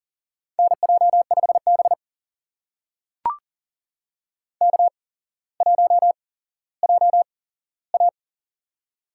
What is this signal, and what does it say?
Morse code, heard as a keyed tone, about 40 wpm, spaced as for 17 wpm, 700 Hz. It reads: NJ5B K1JA
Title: USA Call Signs - 4 Characters - 8x Character Spacing - 40wpm